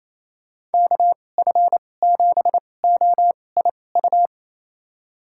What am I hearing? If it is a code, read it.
Morse code, heard as a keyed tone, about 28 wpm, spaced as for 20 wpm, 700 Hz. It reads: KF7OIU
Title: USA Call Signs - 2x Character Spacing - 28wpm